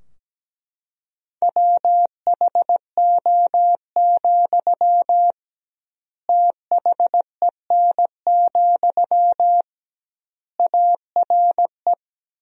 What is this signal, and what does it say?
Morse code, heard as a keyed tone, about 17 wpm, 700 Hz. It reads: WHO, THEN, ARE
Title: 2x Word-Spacing - Sentences from Top 100 Words - 17wpm